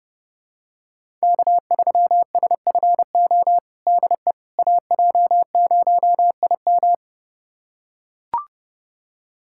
Morse code, heard as a keyed tone, about 32 wpm, 700 Hz. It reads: K3SFO DE AJ0IM
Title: USA Callsign DE Callsign - 5 Characters - 30wpm